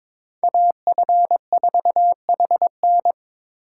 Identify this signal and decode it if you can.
Morse code, heard as a keyed tone, about 26 wpm, 700 Hz. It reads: AF4HN
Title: Speed Racing - USA 2x2 Call Signs 22wpm